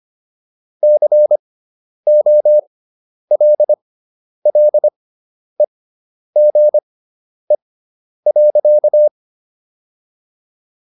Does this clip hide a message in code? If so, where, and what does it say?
Yes: Morse code from 0.8 to 9.1 s, heard as a keyed tone, about 25 wpm, spaced as for 10 wpm, 600 Hz. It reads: COLLEGE.